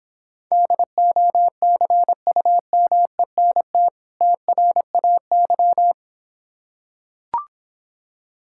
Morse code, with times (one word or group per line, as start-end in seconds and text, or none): 0.5-3.9 DOCUMENT
4.2-5.9 TRAY